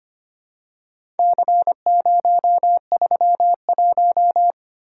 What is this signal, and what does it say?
Morse code, heard as a keyed tone, about 25 wpm, 700 Hz. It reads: C031
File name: practice / 25wpm / Four Letter-Number Combinations - 25wpm